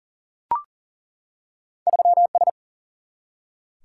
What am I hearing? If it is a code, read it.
Morse code, heard as a keyed tone, about 40 wpm, 700 Hz. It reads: 3S
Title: Single-Letter Single-Number 40wpm